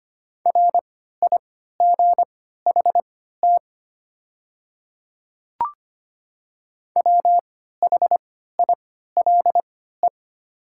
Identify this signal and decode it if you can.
Morse code, heard as a keyed tone, about 25 wpm, spaced as for 14 wpm, 700 Hz. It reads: RIGHT WHILE
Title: 3x Character Spacing - Top 300 Words - 25wpm